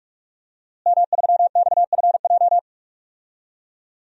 Morse code, heard as a keyed tone, about 47 wpm, 700 Hz. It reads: M3XFJ
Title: DXCC-100 Call Signs - 5 Characters - Repeated 2x - 45wpm